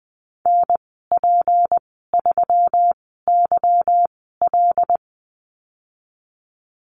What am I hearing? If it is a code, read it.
Morse code, heard as a keyed tone, about 20 wpm, spaced as for 15 wpm, 700 Hz. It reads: NP3YL
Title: 161 Lesson 26g - Letter g - Call Signs 20wpm